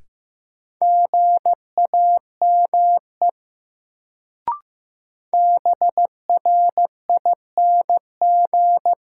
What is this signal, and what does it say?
Morse code, heard as a keyed tone, about 15 wpm, 700 Hz. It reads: GAME BRING
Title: Sentences from Top 500 Words - New Word Drill - 15wpm